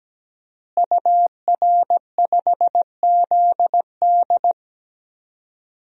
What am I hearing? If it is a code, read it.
Morse code, heard as a keyed tone, about 17 wpm, 700 Hz. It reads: UR5ZD
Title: DXCC-100 Call Signs - 5 Characters - Repeated 2x - 17wpm